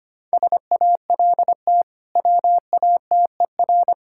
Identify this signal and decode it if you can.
Morse code, heard as a keyed tone, about 25 wpm, 700 Hz. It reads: SALT WATER
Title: Sentences from Top 1000 Words - Review Part 5 - 25wpm